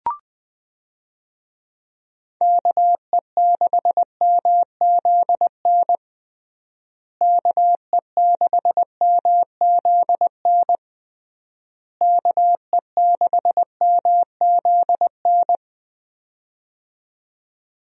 Morse code, heard as a keyed tone, about 20 wpm, 700 Hz. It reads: KE6MZN KE6MZN KE6MZN
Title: USA Call Signs - 6 Characters - Repeated 3x - 20wpm